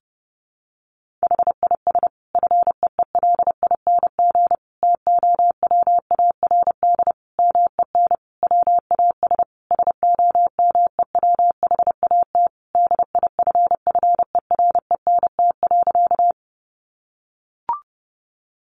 Morse code, heard as a keyed tone, about 30 wpm, 700 Hz. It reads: HIS FEELING TOWARD MEN WAS SOMEWHAT DIFFERENT.
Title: Sentences from Top 1300 Words 30wpm